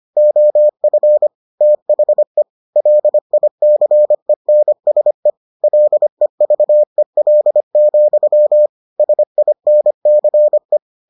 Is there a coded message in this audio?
Yes: Morse code from 0.2 to 10.8 s, heard as a keyed tone, about 25 wpm, 600 Hz. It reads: OF THE LICENSE LEVEL, SINCE